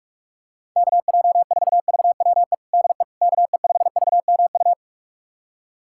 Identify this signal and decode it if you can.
Morse code, heard as a keyed tone, about 46 wpm, 700 Hz. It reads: KJ4VWE DE KE5VMU
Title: USA Callsign DE Callsign - 6 Characters - 45wpm